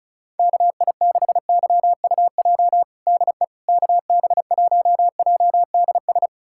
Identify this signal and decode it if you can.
Morse code, heard as a keyed tone, about 36 wpm, 700 Hz. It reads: KI6YUJ DE KB1JDS
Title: USA Callsign DE Callsign 35wpm